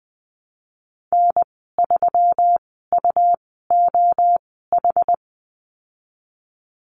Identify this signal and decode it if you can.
Morse code, heard as a keyed tone, about 20 wpm, spaced as for 15 wpm, 700 Hz. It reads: N3UOH
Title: 128 Lesson 21g - Number 6 - Call Signs 20wpm